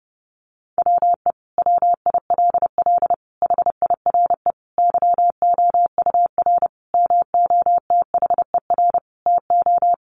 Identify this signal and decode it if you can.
Morse code, heard as a keyed tone, about 30 wpm, 700 Hz. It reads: WE WILL HIRE YOUR MOTHER TO
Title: Sentences from Top 1400 Words 30wpm